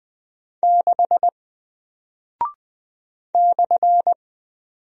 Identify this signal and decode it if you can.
Morse code, heard as a keyed tone, about 20 wpm, 700 Hz. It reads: 6 /
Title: Single Character (Letter:Number:Punctuation:Prosign) Alt 20wpm